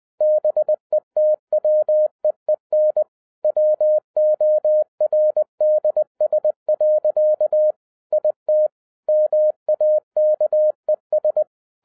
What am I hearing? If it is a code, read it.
Morse code, heard as a keyed tone, about 20 wpm, 610 Hz. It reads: BETWEEN WORDS. IT MAKES
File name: 2011-10-mp3 / n13-TIPS FROM MY 34 YEARS OF EXPERIENCE 20wpm